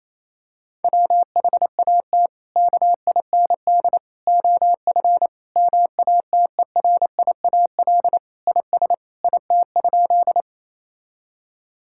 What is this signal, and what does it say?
Morse code, heard as a keyed tone, about 28 wpm, 700 Hz. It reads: WHAT KIND OF MATERIAL IS IT?
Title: Sentences from Top 1000 Words - Review Part 5 - 28wpm